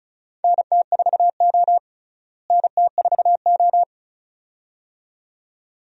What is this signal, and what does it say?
Morse code, heard as a keyed tone, about 34 wpm, 700 Hz. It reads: NT4O NT4O
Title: USA Call Signs - 4 Characters - Repeated 2x - 35wpm